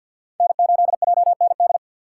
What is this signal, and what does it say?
Morse code, heard as a keyed tone, about 48 wpm, 690 Hz. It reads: N8JND
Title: USA Call Signs - 5 Characters - 4x Character Spacing - 50wpm